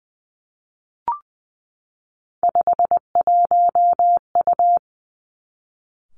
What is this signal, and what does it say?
Morse code, heard as a keyed tone, about 20 wpm, 700 Hz. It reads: 51U
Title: Three Challenging Letter-Number Combinations 20wpm